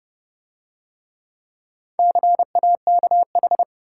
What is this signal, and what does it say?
Morse code, heard as a keyed tone, about 29 wpm, 700 Hz. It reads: CAKH